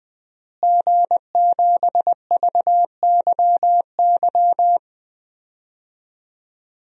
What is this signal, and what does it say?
Morse code, heard as a keyed tone, about 21 wpm, 700 Hz. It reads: G7VYY
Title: DXCC-100 Call Signs - 5 Characters - Repeated 2x - 20wpm